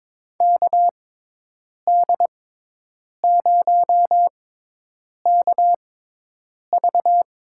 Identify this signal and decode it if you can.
Morse code, heard as a keyed tone, about 22 wpm, spaced as for 8 wpm, 700 Hz. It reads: KD0KV
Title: USA Call Signs - 6 Characters - 6x Character Spacing - 22wpm